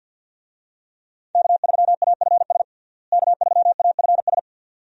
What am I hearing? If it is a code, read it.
Morse code, heard as a keyed tone, about 47 wpm, 700 Hz. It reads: K3AFS K3AFS
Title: USA Call Signs - 5 Characters - Repeated 2x - 50wpm